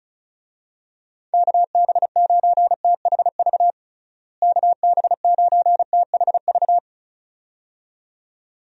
Morse code, heard as a keyed tone, about 36 wpm, 700 Hz. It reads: KB9THV KB9THV
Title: USA Call Signs - 6 Characters - Repeated 2x - 35wpm